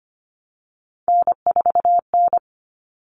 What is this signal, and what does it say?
Morse code, heard as a keyed tone, about 25 wpm, 700 Hz. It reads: N4N
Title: All Cut Numbers - 3-Digit 25wpm